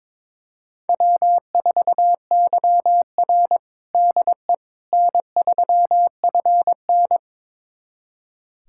Morse code, heard as a keyed tone, about 22 wpm, 700 Hz. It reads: W4YR DE N3FN
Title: USA Callsign DE Callsign - 4 Characters - 22wpm